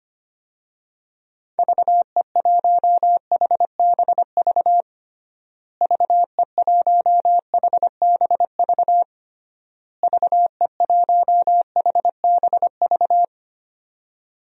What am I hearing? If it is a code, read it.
Morse code, heard as a keyed tone, about 25 wpm, 700 Hz. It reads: VE1HBV VE1HBV VE1HBV
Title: DXCC-100 Call Signs - 6 Characters - Repeated 3x - 25wpm